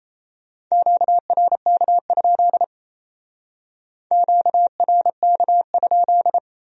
Morse code, heard as a keyed tone, about 35 wpm, 700 Hz. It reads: QRK? QRK?
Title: Speed-Racing Q-Codes - 28wpm